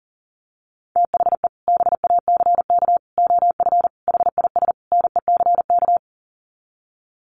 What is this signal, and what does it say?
Morse code, heard as a keyed tone, about 40 wpm, 700 Hz. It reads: THE BACK OF HIS NECK
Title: Sets of 5 Words Encore 40wpm